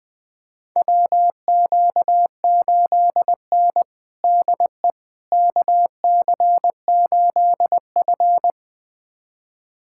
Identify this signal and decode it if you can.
Morse code, heard as a keyed tone, about 20 wpm, 700 Hz. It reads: WQ8N DE KC8F
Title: USA Callsign DE Callsign - 4 Characters - 20wpm